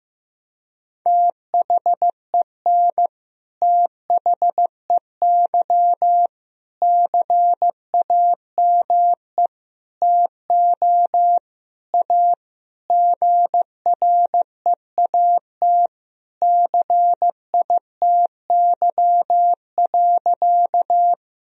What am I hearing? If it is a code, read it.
Morse code, heard as a keyed tone, about 15 wpm, 700 Hz. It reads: THEN THEY CAME TO A GREAT CITY.
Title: Sentences from Top 1000 Words - Review Part 1 - 15wpm